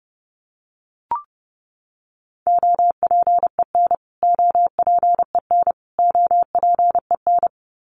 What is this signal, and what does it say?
Morse code, heard as a keyed tone, about 30 wpm, 700 Hz. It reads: OPEN OPEN OPEN
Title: Top 500 Words Repeated 30wpm